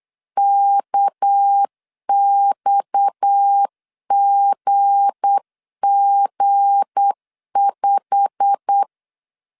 Morse code, heard as a keyed tone, about 8 wpm, 800 Hz. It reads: KXGG5
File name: LEZIONI / 9 28 8wpm 3512